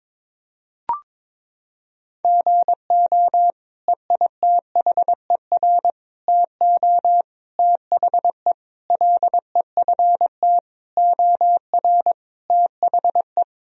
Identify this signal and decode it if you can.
Morse code, heard as a keyed tone, about 22 wpm, 700 Hz. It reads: GO EITHER TO THE LEFT OR THE
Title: Sentences from Top 1000 Words - Review Part 4 - 22wpm